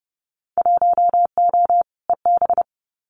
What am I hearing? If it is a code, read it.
Morse code, heard as a keyed tone, about 30 wpm, 700 Hz. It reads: 1O EB
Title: Winter Field Day Exchanges 30wpm